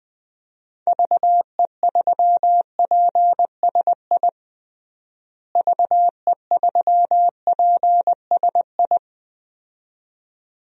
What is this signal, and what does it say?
Morse code, heard as a keyed tone, about 21 wpm, 700 Hz. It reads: VE3PSI VE3PSI
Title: DXCC-100 Call Signs - 6 Characters - Repeated 2x - 20wpm